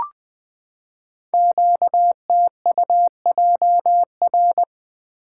Morse code, heard as a keyed tone, about 20 wpm, 700 Hz. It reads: QTUJR
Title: Code Groups - 5 Letters - 20wpm